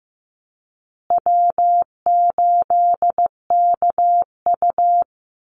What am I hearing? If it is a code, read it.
Morse code, heard as a keyed tone, about 15 wpm, 700 Hz. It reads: W8KU
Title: USA Call Signs Repeated 2x 15wpm